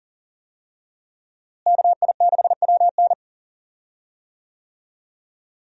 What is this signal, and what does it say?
Morse code, heard as a keyed tone, about 40 wpm, 700 Hz. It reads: KI6WN
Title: USA Call Signs - 5 Characters - Repeated 2x - 40wpm